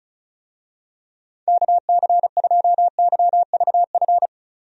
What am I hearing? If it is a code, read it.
Morse code, heard as a keyed tone, about 36 wpm, 700 Hz. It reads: KC2YVF
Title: USA Call Signs - 6 Characters - 35wpm